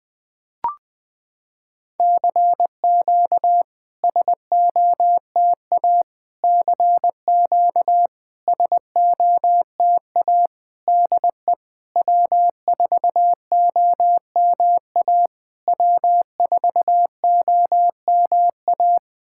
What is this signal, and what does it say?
Morse code, heard as a keyed tone, about 21 wpm, 690 Hz. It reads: CQ SOTA CQ SOTA DE W4OMA W4OMA
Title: SOTA QSO Practice 20wpm